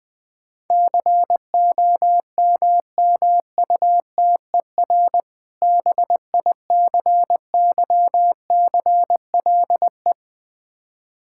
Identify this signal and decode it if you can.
Morse code, heard as a keyed tone, about 20 wpm, 700 Hz. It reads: COMMUTER BICYCLE
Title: Difficult Words - Speed-Racing with Hints - 20wpm